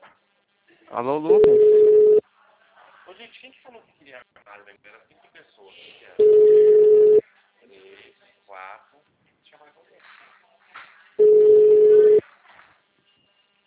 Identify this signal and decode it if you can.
Ringback tone (Europe, 425 Hz), with speech.